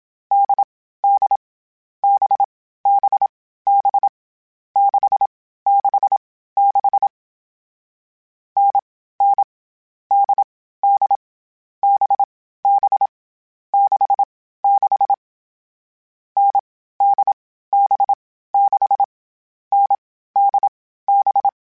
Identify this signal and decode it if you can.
Morse code, heard as a keyed tone, about 27 wpm, 800 Hz. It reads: D D B B B 6 6 6 N N D D B B 6 6 N D B 6 N D B
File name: Difficult Combos / Difficult Combos-mp3s / NDB NDB6 @25